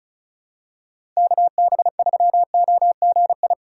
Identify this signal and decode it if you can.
Morse code, heard as a keyed tone, about 36 wpm, 700 Hz. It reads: KB3OGI
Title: USA Call Signs - 6 Characters - 4x Character Spacing - 35wpm